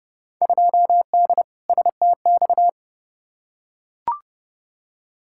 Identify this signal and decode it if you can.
Morse code, heard as a keyed tone, about 31 wpm, 700 Hz. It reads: 2D STX
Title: Field Day Exchanges - 30wpm